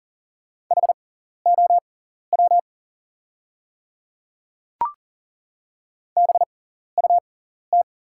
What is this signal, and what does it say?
Morse code, heard as a keyed tone, about 39 wpm, spaced as for 14 wpm, 700 Hz. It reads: HOW BUT
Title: 6x Character Spacing - Top 300 Words - 40wpm